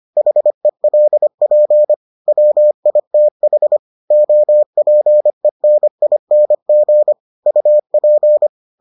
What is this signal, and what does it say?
Morse code, heard as a keyed tone, about 25 wpm, 600 Hz. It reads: HELP WITH OPENING UP